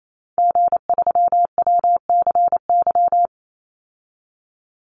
Morse code, heard as a keyed tone, about 28 wpm, 700 Hz. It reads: G3WCY
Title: Call Signs Repeated 3x - 28wpm